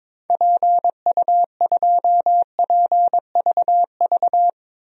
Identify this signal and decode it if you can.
Morse code, heard as a keyed tone, about 21 wpm, 700 Hz. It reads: PU2PVV
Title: DXCC-100 Call Signs - 6 Characters - 22wpm